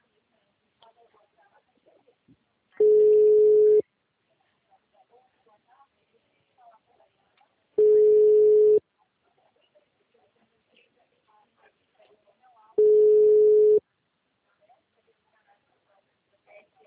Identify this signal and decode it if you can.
Ringback tone (Europe, 425 Hz), with speech.